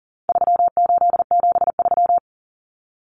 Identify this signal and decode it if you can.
Morse code, heard as a keyed tone, about 40 wpm, 700 Hz. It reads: 3873
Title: Numbers - 4-Digit 40wpm